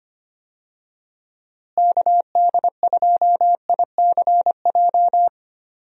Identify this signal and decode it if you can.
Morse code, heard as a keyed tone, about 25 wpm, 700 Hz. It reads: KD2ICJ
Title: USA Call Signs - 6 Characters - Repeated 3x - 25wpm